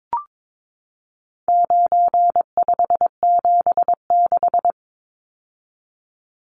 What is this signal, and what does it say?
Morse code, heard as a keyed tone, about 22 wpm, 700 Hz. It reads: 9576
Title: Numbers - 4-Digit 22wpm